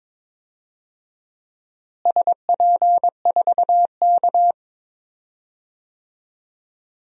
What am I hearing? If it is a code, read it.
Morse code, heard as a keyed tone, about 22 wpm, 700 Hz. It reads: SP4K
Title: DXCC-100 Call Signs - 4 Characters - Repeated 3x - 22wpm